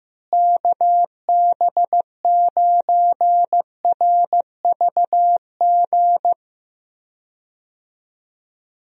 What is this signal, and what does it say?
Morse code, heard as a keyed tone, about 15 wpm, 700 Hz. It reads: KB9RVG
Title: USA Call Signs - 6 Characters - Repeated 2x - 15wpm